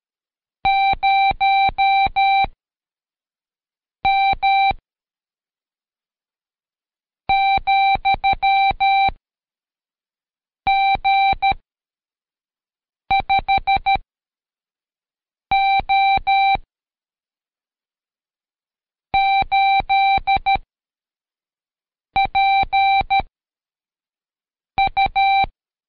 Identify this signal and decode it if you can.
Morse code, heard as a keyed tone, about 13 wpm, 780 Hz. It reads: 0 M , G 5 O 8 P U
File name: Code Quick / AudioDownld2 / Seg104-Day12-FiinalExam 1